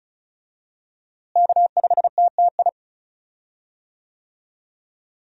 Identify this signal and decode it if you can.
Morse code, heard as a keyed tone, about 35 wpm, 700 Hz. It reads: K5TTI
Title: USA Call Signs - 5 Characters - Repeated 3x - 35wpm